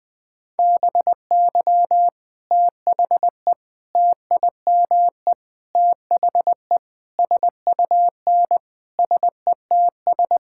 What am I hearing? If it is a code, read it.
Morse code, heard as a keyed tone, about 20 wpm, 700 Hz. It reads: BY THE TIME THE SUN SETS
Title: Sentences with 4-Word Phrases - 20wpm